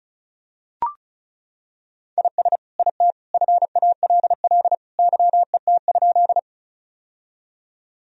Morse code, heard as a keyed tone, about 34 wpm, 700 Hz. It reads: IS IT FALL YET?
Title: Sentences from Top 1000 Words - Review Part 2 - 35wpm